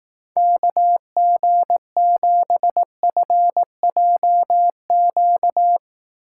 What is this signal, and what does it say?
Morse code, heard as a keyed tone, about 18 wpm, 700 Hz. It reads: KG7FJQ
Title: Speed Racing - USA Call Signs 15wpm